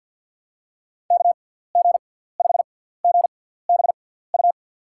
Morse code, heard as a keyed tone, about 53 wpm, spaced as for 17 wpm, 700 Hz. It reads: KG5GBU
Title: USA Call Signs - 6 Characters - 6x Character Spacing - 50wpm